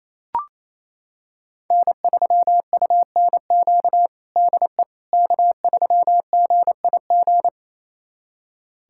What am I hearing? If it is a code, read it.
Morse code, heard as a keyed tone, about 28 wpm, 690 Hz. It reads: N3UNQ DE K3GIG
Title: USA Callsign DE Callsign - 5 Characters - 28wpm